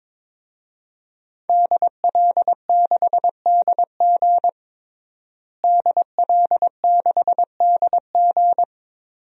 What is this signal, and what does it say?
Morse code, heard as a keyed tone, about 22 wpm, 700 Hz. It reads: DL6DG DL6DG
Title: DXCC-100 Call Signs - 5 Characters - Repeated 3x - 22wpm